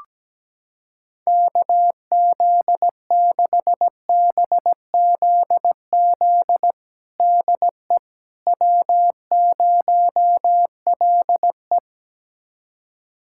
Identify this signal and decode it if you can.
Morse code, heard as a keyed tone, about 17 wpm, 700 Hz. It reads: KZ6BZZ DE W0LE
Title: USA Callsign DE Callsign 17wpm